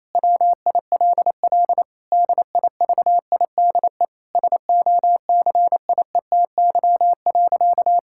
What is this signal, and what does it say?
Morse code, heard as a keyed tone, about 27 wpm, 700 Hz. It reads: WILL DIVIDE SOCIETY.
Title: Sentences from Top 1600 Words 28wpm